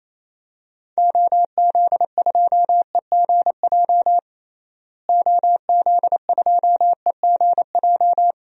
Morse code, heard as a keyed tone, about 28 wpm, 700 Hz. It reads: OZ2EGJ OZ2EGJ
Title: DXCC-100 Call Signs - 6 Characters - Repeated 2x - 28wpm